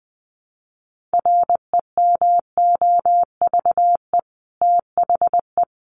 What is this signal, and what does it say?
Morse code, heard as a keyed tone, about 20 wpm, 700 Hz. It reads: REMOVE THE
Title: Sentences from Top 1300 Words 20wpm